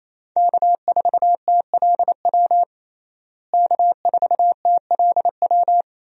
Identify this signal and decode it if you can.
Morse code, heard as a keyed tone, about 28 wpm, 700 Hz. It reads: K4TLW K4TLW
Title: USA Call Signs - 5 Characters - Repeated 2x - 28wpm